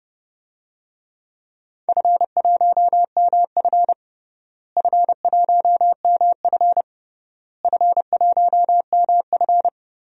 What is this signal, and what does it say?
Morse code, heard as a keyed tone, about 30 wpm, 700 Hz. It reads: F1MF F1MF F1MF
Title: DXCC-100 Call Signs - 4 Characters - Repeated 3x - 30wpm